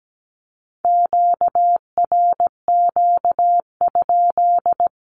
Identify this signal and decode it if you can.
Morse code, heard as a keyed tone, about 17 wpm, 700 Hz. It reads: QRQ?
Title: Q-Codes with Spoken Meaning - 17wpm